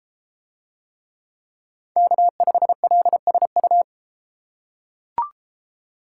Morse code, heard as a keyed tone, about 31 wpm, 700 Hz. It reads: K5LSU